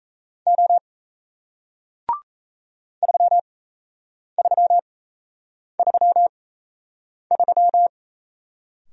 Morse code, heard as a keyed tone, about 37 wpm, spaced as for 8 wpm, 700 Hz. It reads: O 3333